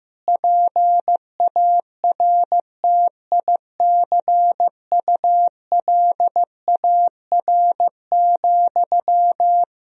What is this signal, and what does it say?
Morse code, heard as a keyed tone, about 15 wpm, 700 Hz. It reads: PARTICULAR,
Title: Sentences from Top 1000 Words - Review Part 4 - 15wpm